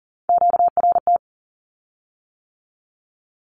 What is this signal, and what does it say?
Morse code, heard as a keyed tone, about 40 wpm, 700 Hz. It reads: QRT
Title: Q-Codes with Spoken Meaning - 40wpm